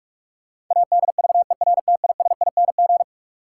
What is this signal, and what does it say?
Morse code, heard as a keyed tone, about 46 wpm, 700 Hz. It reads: ADVERTISING
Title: Sentences from Top 1000 Words - Review Part 6 - New Word Drill - 45wpm